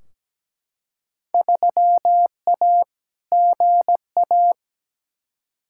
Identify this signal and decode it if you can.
Morse code, heard as a keyed tone, about 17 wpm, 700 Hz. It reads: 3A GA
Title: Field Day Exchanges Encore - 17wpm